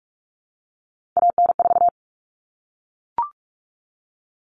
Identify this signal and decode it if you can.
Morse code, heard as a keyed tone, about 44 wpm, 700 Hz. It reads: AN4
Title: All Cut Numbers - 3-Digit 45wpm